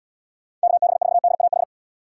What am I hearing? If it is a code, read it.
Morse code, heard as a keyed tone, about 73 wpm, 700 Hz. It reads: KB4DAH